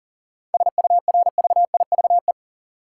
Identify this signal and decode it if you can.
Morse code, heard as a keyed tone, about 39 wpm, 700 Hz. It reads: SURVIVE
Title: Sentences from Top 1300 Words - New Word Drill - 40wpm